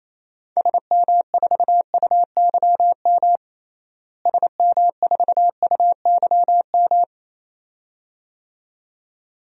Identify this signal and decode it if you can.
Morse code, heard as a keyed tone, about 28 wpm, 700 Hz. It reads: SM4UYM SM4UYM
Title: DXCC-100 Call Signs - 6 Characters - Repeated 2x - 28wpm